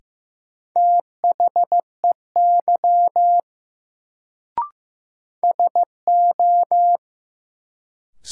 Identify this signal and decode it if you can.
Morse code, heard as a keyed tone, about 15 wpm, 700 Hz. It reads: THEY SO
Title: Sets of 2 Words - New Word Drill - 15wpm